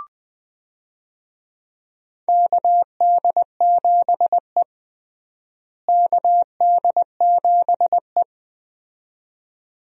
Morse code, heard as a keyed tone, about 20 wpm, 700 Hz. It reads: KD7E KD7E
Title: USA Call Signs - 4 Characters - Repeated 2x - 20wpm